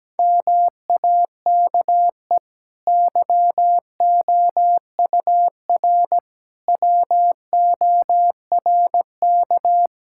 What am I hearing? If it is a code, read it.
Morse code, heard as a keyed tone, about 17 wpm, 700 Hz. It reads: MAKE YOUR WORK
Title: Sentences from Top 1000 Words - Review Part 6 - 17wpm